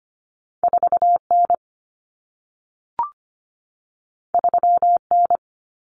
Morse code, heard as a keyed tone, about 25 wpm, 700 Hz. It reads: 4N 3N
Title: Common Cut Numbers - 2-Digit 25wpm